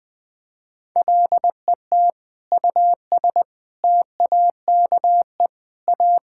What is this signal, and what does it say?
Morse code, heard as a keyed tone, about 21 wpm, 700 Hz. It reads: LET US TAKE A
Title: Sentences with 6-Word Phrases - 20wpm